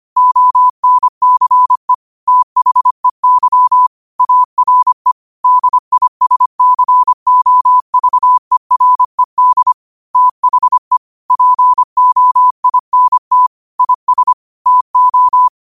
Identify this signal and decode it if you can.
Morse code, heard as a keyed tone, about 25 wpm, 1000 Hz. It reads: ONCE THEY ARE DISCOVERED THE POINT IS TO
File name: qotdfiles / 25wpm